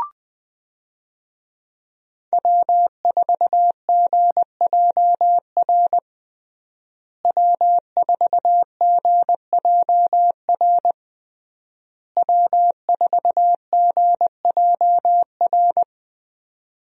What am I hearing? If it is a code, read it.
Morse code, heard as a keyed tone, about 20 wpm, 700 Hz. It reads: W4GJR W4GJR W4GJR
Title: USA Call Signs - 5 Characters - Repeated 3x - 20wpm